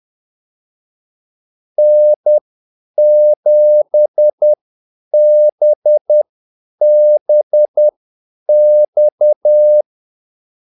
Morse code, heard as a keyed tone, about 10 wpm, spaced as for 8 wpm, 600 Hz. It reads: N7BBX
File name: Callsigns-10-8wpm